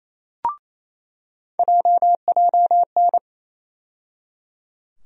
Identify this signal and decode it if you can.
Morse code, heard as a keyed tone, about 28 wpm, 700 Hz. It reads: JJN
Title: USA Call Sign Postfixes - 3 Letters 28wpm